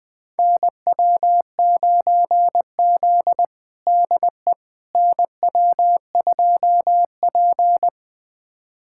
Morse code, heard as a keyed tone, about 20 wpm, 700 Hz. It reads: NW9Z DE NW2P
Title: USA Callsign DE Callsign - 4 Characters - 20wpm